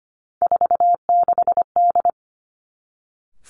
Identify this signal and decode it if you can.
Morse code, heard as a keyed tone, about 25 wpm, 700 Hz. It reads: 46D
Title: Three Challenging Letter-Number Combinations 25wpm